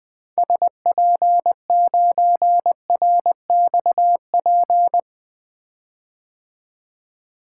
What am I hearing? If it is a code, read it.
Morse code, heard as a keyed tone, about 20 wpm, 700 Hz. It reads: SP9RXP
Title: 2023 POTA Hunter Call Signs - 20wpm